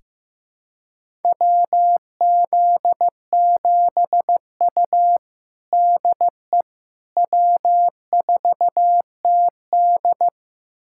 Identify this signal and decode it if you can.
Morse code, heard as a keyed tone, about 16 wpm, 700 Hz. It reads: WZ7U DE W4TD
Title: USA Callsign DE Callsign - 4 Characters - 15wpm